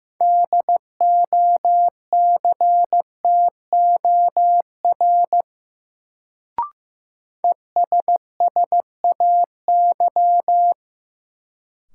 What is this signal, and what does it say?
Morse code, heard as a keyed tone, about 15 wpm, 700 Hz. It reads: DOCTOR ESSAY
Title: Sentences from Top 1500 Words - New Word Drill Alt - 15wpm